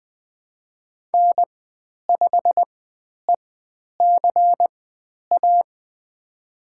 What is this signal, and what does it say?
Morse code, heard as a keyed tone, about 20 wpm, spaced as for 10 wpm, 700 Hz. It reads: N5ECA
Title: 095 Lesson 16g - Letter c - Call Signs 20wpm